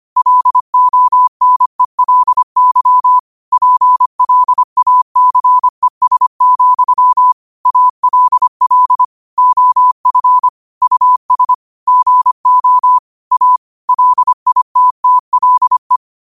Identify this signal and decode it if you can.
Morse code, heard as a keyed tone, about 25 wpm, 1000 Hz. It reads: LONELY PLACES, ALL OF US GO A LITTLE